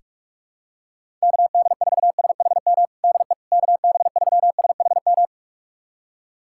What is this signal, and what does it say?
Morse code, heard as a keyed tone, about 42 wpm, 700 Hz. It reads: KD4SHM DE KB3SHM
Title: USA Callsign DE Callsign - 6 Characters - 45wpm